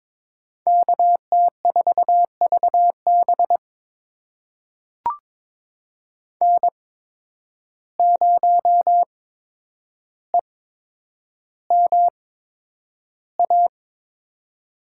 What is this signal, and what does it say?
Morse code, heard as a keyed tone, about 22 wpm, spaced as for 9 wpm, 700 Hz. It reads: KT4VB N0EMA
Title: USA Call Signs - 5 Characters - 8x Character Spacing - 22wpm